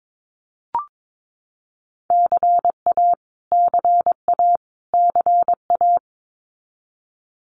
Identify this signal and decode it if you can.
Morse code, heard as a keyed tone, about 22 wpm, 700 Hz. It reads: CA CA CA
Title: US States Repeated 3x 22wpm